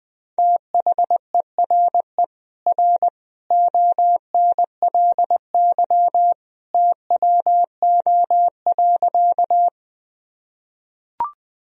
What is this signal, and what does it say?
Morse code, heard as a keyed tone, about 20 wpm, 700 Hz. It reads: THERE R ONLY TWO.
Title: Sentences from Top 200 Words and Abbreviations 20wpm